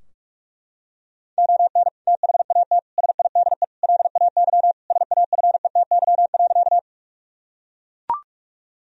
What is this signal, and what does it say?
Morse code, heard as a keyed tone, about 43 wpm, 700 Hz. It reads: ON THAT SIDE LAY SAFETY.
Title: Sentences from Top 1000 Words - Review Part 4 - 45wpm